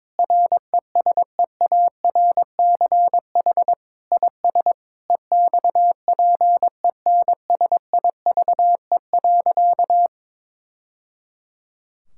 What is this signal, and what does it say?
Morse code, heard as a keyed tone, about 21 wpm, 700 Hz. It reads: RESEARCH IS EXPENSIVE.
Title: Sentences from Top 1000 Words - Review Part 6 - 22wpm